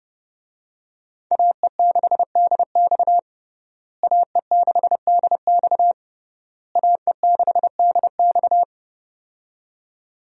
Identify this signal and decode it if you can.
Morse code, heard as a keyed tone, about 32 wpm, 700 Hz. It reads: AE6DX AE6DX AE6DX
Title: USA Call Signs - 5 Characters - Repeated 3x - 30wpm